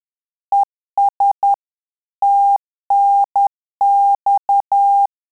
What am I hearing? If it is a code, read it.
Morse code, heard as a keyed tone, about 10 wpm, 800 Hz. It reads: ES TNX
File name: Telegraphy-group / QSO CW 50 CPM